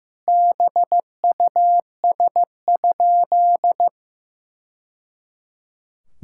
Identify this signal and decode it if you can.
Morse code, heard as a keyed tone, about 15 wpm, 700 Hz. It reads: BUS?
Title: Sentences from Top 1000 Words - Review Part 6 - 15wpm